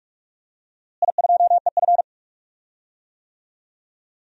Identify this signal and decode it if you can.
Morse code, heard as a keyed tone, about 42 wpm, 700 Hz. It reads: I2EF
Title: DXCC-100 Call Signs - 4 Characters - Repeated 3x - 45wpm